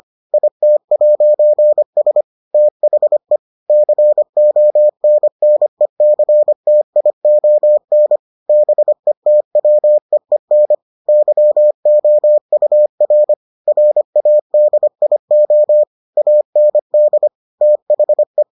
Morse code, heard as a keyed tone, about 25 wpm, 600 Hz. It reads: IT'S THE CONNECTION BETWEEN YOUR RADIO AND THE